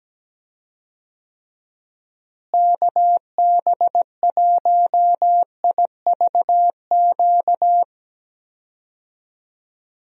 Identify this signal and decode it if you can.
Morse code, heard as a keyed tone, about 17 wpm, 700 Hz. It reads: KB1IVQ
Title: USA Call Signs - 6 Characters - Repeated 3x - 17wpm